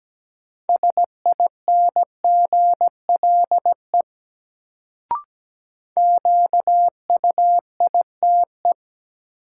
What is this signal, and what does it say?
Morse code, heard as a keyed tone, about 17 wpm, 700 Hz. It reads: SINGLE QUITE
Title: Sentences from Top 700 Words - New Word Drill - 17wpm